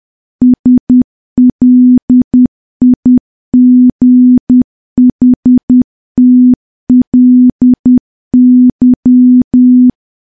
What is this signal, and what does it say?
Morse code, heard as a keyed tone, about 10 wpm, 250 Hz. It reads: SLIGHTLY
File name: morse / morse-training-10WPM